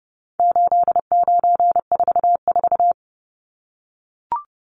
Morse code, heard as a keyed tone, about 30 wpm, 700 Hz. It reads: 8944
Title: Numbers - 4-Digit 30wpm